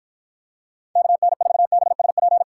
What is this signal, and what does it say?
Morse code, heard as a keyed tone, about 53 wpm, 700 Hz. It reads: KN4BSP